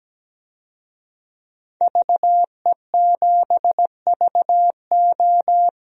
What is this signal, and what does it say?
Morse code, heard as a keyed tone, about 17 wpm, 700 Hz. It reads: VE7VO
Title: DXCC-100 Call Signs - 5 Characters - 17wpm